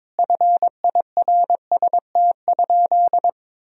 Morse code, heard as a keyed tone, about 22 wpm, 700 Hz. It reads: FIRST?
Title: Sentences from Top 1000 Words - Review Part 3 - 22wpm